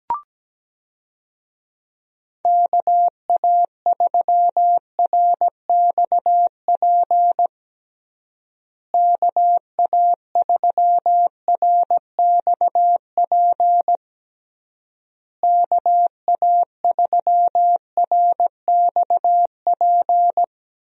Morse code, heard as a keyed tone, about 17 wpm, 700 Hz. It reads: KA3RXP KA3RXP KA3RXP
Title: USA Call Signs - 6 Characters - Repeated 3x - 17wpm